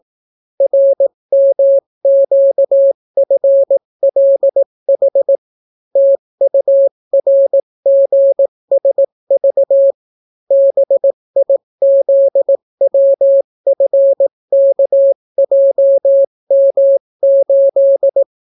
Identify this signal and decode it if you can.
Morse code, heard as a keyed tone, about 18 wpm, 550 Hz. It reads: RMQFLH TURGSV BIZWFKJM8